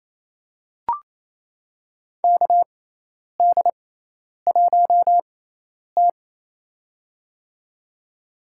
Morse code, heard as a keyed tone, about 28 wpm, spaced as for 10 wpm, 700 Hz. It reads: KD1T
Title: USA Call Signs - 4 Characters - 6x Character Spacing - 28wpm